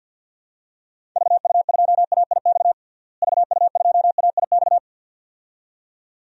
Morse code, heard as a keyed tone, about 51 wpm, 700 Hz. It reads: VU2AIX VU2AIX
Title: DXCC-100 Call Signs - 6 Characters - Repeated 2x - 50wpm